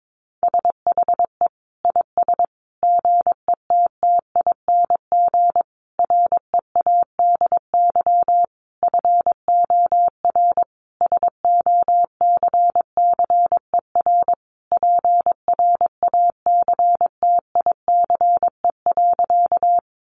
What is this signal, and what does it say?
Morse code, heard as a keyed tone, about 22 wpm, 700 Hz. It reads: SHE IS GETTING READY FOR SOCCER PRACTICE.
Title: Easy Phrases 22wpm